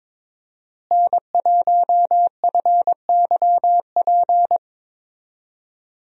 Morse code, heard as a keyed tone, about 21 wpm, 700 Hz. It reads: N1FYP
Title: USA Call Signs - 5 Characters - Repeated 2x - 22wpm